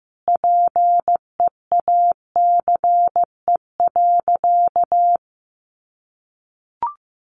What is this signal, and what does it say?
Morse code, heard as a keyed tone, about 15 wpm, 700 Hz. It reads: PEACE.
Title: A Sound Of Thunder 15wpm